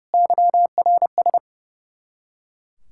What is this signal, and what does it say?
Morse code, heard as a keyed tone, about 30 wpm, 700 Hz. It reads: YRS
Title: Practice QSO - Exchange 3 - More Info - 30wpm